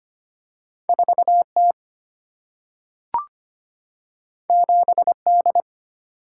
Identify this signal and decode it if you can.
Morse code, heard as a keyed tone, about 25 wpm, 700 Hz. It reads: 4T 7D
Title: Single-Letter Single-Number 25wpm